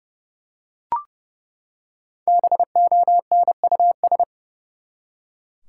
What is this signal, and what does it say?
Morse code, heard as a keyed tone, about 32 wpm, 700 Hz. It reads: BONUS
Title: Five Letter Words - 30wpm